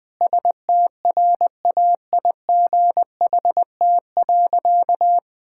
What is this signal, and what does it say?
Morse code, heard as a keyed tone, about 21 wpm, 700 Hz. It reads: STRAIGHT.
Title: Sentences from Top 1000 Words - Review Part 6 - 20wpm